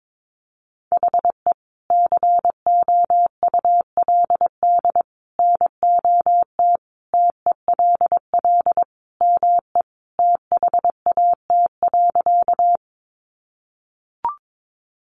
Morse code, heard as a keyed tone, about 22 wpm, 700 Hz. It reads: HE COULD NOT TELL ME THAT.
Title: Sentences from Top 600 Words 22wpm